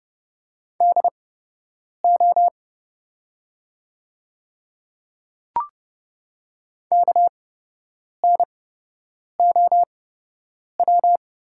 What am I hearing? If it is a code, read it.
Morse code, heard as a keyed tone, about 30 wpm, spaced as for 8 wpm, 700 Hz. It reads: DO KNOW
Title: 8x Character Spacing - Top 100 Words - 30wpm